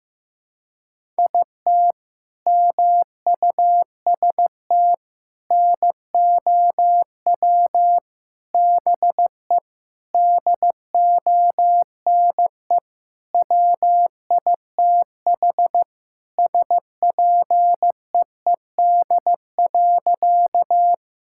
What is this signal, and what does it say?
Morse code, heard as a keyed tone, about 16 wpm, 700 Hz. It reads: IT MUST NOW BE DONE WITH SPEED.
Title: Sentences from Top 1000 Words - Review Part 5 - 15wpm